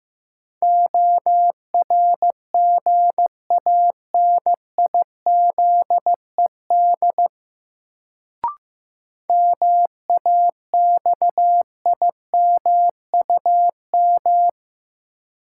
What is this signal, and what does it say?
Morse code, heard as a keyed tone, about 15 wpm, 700 Hz. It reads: ORGANIZED MAXIMUM
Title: Sentences from Top 1300 Words - New Word Drill - 15wpm